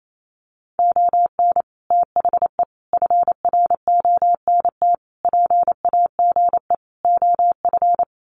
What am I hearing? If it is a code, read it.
Morse code, heard as a keyed tone, about 28 wpm, 690 Hz. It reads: ON THE FRONT PAGE OF
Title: Sets of 5 Words Encore 28wpm